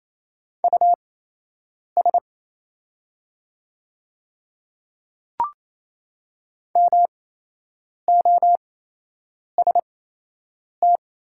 Morse code, heard as a keyed tone, about 28 wpm, spaced as for 8 wpm, 700 Hz. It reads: US MOST
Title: 8x Character Spacing - Top 200 Words - 28wpm